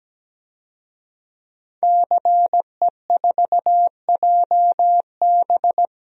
Morse code, heard as a keyed tone, about 17 wpm, 700 Hz. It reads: CE4JB
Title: DXCC-100 Call Signs - 5 Characters - 17wpm